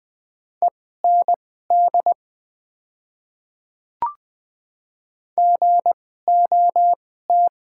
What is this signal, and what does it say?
Morse code, heard as a keyed tone, about 20 wpm, spaced as for 15 wpm, 700 Hz. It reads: END GOT
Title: 2x Character Spacing - Top 500 Words - 20wpm